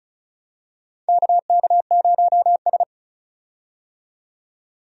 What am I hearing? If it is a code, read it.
Morse code, heard as a keyed tone, about 36 wpm, 700 Hz. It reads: KK0S
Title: USA Call Signs - 4 Characters - 35wpm